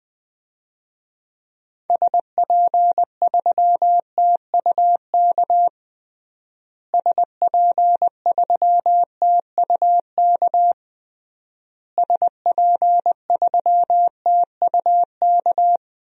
Morse code, heard as a keyed tone, about 20 wpm, 700 Hz. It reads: SP3TUK SP3TUK SP3TUK
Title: DXCC-100 Call Signs - 6 Characters - Repeated 3x - 20wpm